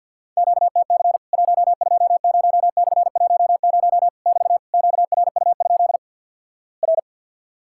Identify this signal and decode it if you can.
Morse code, heard as a keyed tone, about 50 wpm, 700 Hz. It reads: QTC 120/10 <BT> QRU? R
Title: Worked All Europe DX Contest - QTC Exchange - 50wpm